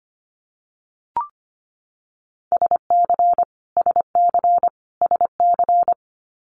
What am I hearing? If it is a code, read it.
Morse code, heard as a keyed tone, about 25 wpm, 700 Hz. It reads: SC SC SC
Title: Multi-letter Phonemes Repeated 3x 25wpm